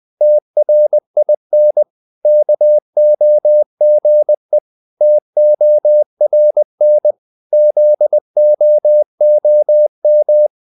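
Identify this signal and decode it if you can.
Morse code, heard as a keyed tone, about 20 wpm, 600 Hz. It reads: TRIN KOGE TORN ZOOM
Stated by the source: Morse code practice of short Danish words at 20 wpm, from CW med Gnister.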